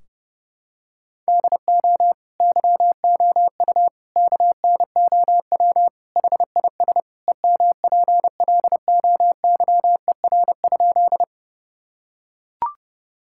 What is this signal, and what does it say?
Morse code, heard as a keyed tone, about 30 wpm, 700 Hz. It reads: DO YOU KNOW HIS EMPLOYER?
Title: Sentences from Top 1600 Words 30wpm